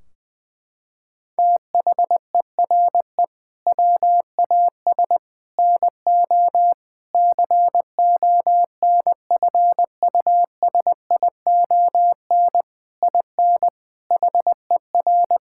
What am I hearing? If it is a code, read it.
Morse code, heard as a keyed tone, about 20 wpm, 700 Hz. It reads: THERE WAS NO CONFUSION IN HER
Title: Sentences from Top 1600 Words 20wpm